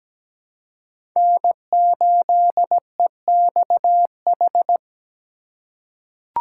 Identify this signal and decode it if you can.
Morse code, heard as a keyed tone, about 17 wpm, 700 Hz. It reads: N8EXH
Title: USA Call Signs - 5 Characters - 3x Character Spacing - 17wpm